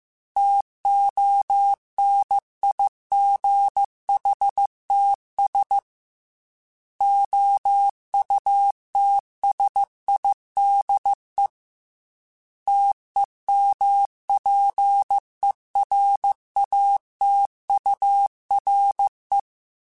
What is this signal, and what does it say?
Morse code, heard as a keyed tone, about 15 wpm, 800 Hz. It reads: TONIGHTS OUTSIDE TEMPERATURE
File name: Weather Sets / Weather Sets-mp3s / Weather 5 5 @13